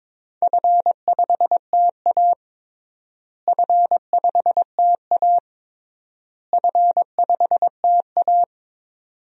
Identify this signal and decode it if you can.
Morse code, heard as a keyed tone, about 22 wpm, 700 Hz. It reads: F5TA F5TA F5TA
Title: DXCC-100 Call Signs - 4 Characters - Repeated 3x - 22wpm